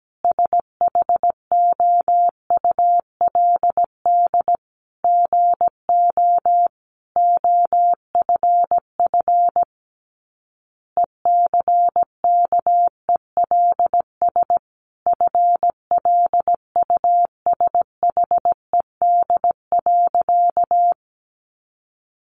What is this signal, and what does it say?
Morse code, heard as a keyed tone, about 17 wpm, 700 Hz. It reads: SHOULD GO OFF ECKELS FLUSHED.
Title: A Sound Of Thunder 17wpm